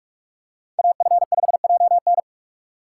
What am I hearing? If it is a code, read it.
Morse code, heard as a keyed tone, about 46 wpm, 700 Hz. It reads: AF5JN
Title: USA Call Signs - 4x Character Spacing - 45wpm